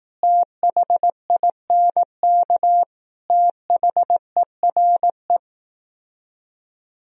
Morse code, heard as a keyed tone, about 18 wpm, 700 Hz. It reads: THINK THERE
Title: Speed-Racing - Sets of 5 Words 15wpm